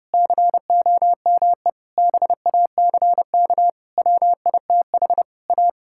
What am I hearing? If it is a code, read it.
Morse code, heard as a keyed tone, about 30 wpm, 700 Hz. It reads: COME BACK WITH A
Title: Sentences from Top 1000 Words - Review Part 5 - 30wpm